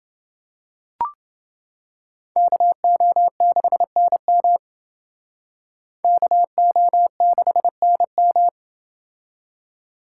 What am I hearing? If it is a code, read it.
Morse code, heard as a keyed tone, about 29 wpm, 700 Hz. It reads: KO6NM KO6NM